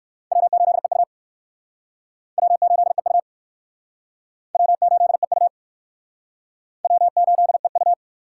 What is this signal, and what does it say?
Morse code, heard as a keyed tone, about 55 wpm, 700 Hz. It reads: W8EU W8EU W8EU W8EU
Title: Speed Racing - USA Call Signs 45wpm